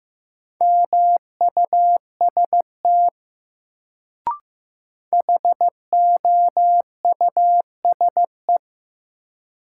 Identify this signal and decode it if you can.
Morse code, heard as a keyed tone, about 16 wpm, 700 Hz. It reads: MUST HOUSE
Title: Sentences from Top 200 Words - New Word Drill - 15wpm